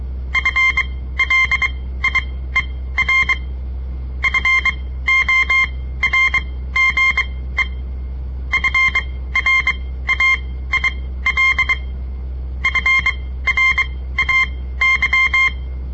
Morse code, heard as a keyed tone, about 23 wpm, spaced as for 15 wpm, 2100 Hz. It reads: FLIER FORGE FRAIL FRAY
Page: On the air Morse code practice from the Southern Montana Amatuer Radio Association; all code, no read back